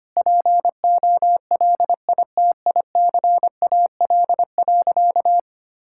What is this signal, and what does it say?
Morse code, heard as a keyed tone, about 25 wpm, 700 Hz. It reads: POLITICAL.
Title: Sentences from Top 1600 Words 25wpm